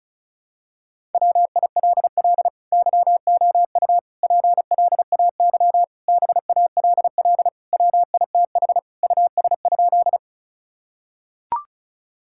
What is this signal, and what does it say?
Morse code, heard as a keyed tone, about 34 wpm, 690 Hz. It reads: WILL YOU PLAY BALL WITH US?
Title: Sentences from Top 1000 Words - Review Part 6 - 35wpm